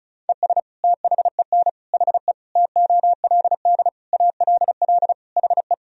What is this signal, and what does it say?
Morse code, heard as a keyed tone, about 36 wpm, 700 Hz. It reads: ES THEN HE TOLD ALL HE
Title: Sentences from Top 400 Words and Abbreviations 35wpm